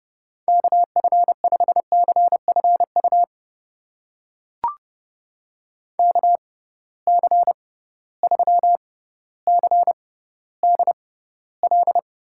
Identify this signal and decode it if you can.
Morse code, heard as a keyed tone, about 30 wpm, spaced as for 15 wpm, 700 Hz. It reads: KF5CFU KC3CDL
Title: USA Call Signs - 6 Characters - 6x Character Spacing - 30wpm